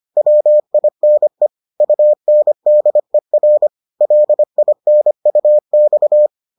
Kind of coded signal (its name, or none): Morse code